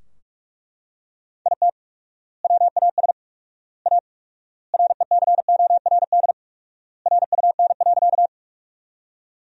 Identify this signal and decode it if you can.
Morse code, heard as a keyed tone, about 41 wpm, 700 Hz. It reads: IT WAS A RECORD RUN.
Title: 4x Word-Spacing - Sentences from Top 500 Words - 45wpm